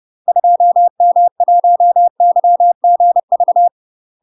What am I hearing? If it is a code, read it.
Morse code, heard as a keyed tone, about 30 wpm, 700 Hz. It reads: 2M1YGV
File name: UK MASTER QSOs TRAINING / 30WPM / UK MASTER QSOs 0033